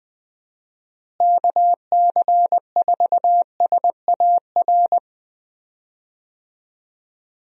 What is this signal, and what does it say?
Morse code, heard as a keyed tone, about 20 wpm, 700 Hz. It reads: KC4SAR
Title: USA Call Signs - 6 Characters - Repeated 2x - 20wpm